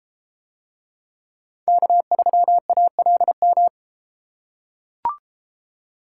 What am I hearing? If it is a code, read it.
Morse code, heard as a keyed tone, about 32 wpm, 700 Hz. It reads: K3ALM